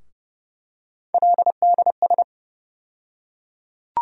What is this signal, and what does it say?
Morse code, heard as a keyed tone, about 30 wpm, 700 Hz. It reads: LDS
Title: USA Call Sign Postfixes - 3 Letters 30wpm